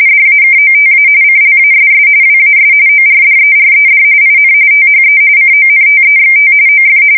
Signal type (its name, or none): RTTY